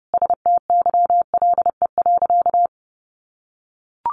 Morse code, heard as a keyed tone, about 30 wpm, 700 Hz. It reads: STYLE.
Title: Sentences from Top 800 Words 30wpm